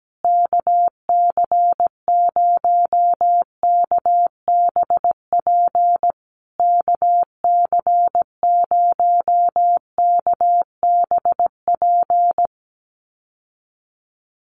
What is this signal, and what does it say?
Morse code, heard as a keyed tone, about 17 wpm, 700 Hz. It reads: KC0KBP KC0KBP
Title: USA Call Signs Repeated 3x 17wpm